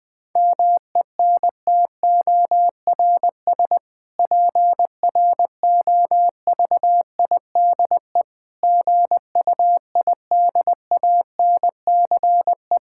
Morse code, heard as a keyed tone, about 20 wpm, 700 Hz. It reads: MENTORS PROVIDE GUIDANCE
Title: Difficult Sentences with Hints - 20wpm